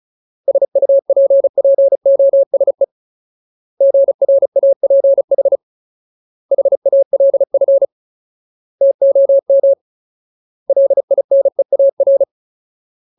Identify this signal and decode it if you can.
Morse code, heard as a keyed tone, about 34 wpm, 550 Hz. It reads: SUPPOSE GRAPH HALF TOM LINEAR
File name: Ultimate Word List / 35WPM Ultimate